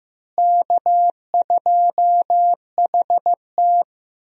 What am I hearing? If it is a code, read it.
Morse code, heard as a keyed tone, about 15 wpm, 700 Hz. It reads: K2HT
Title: USA Call Signs - 4 Characters - Repeated 3x - 15wpm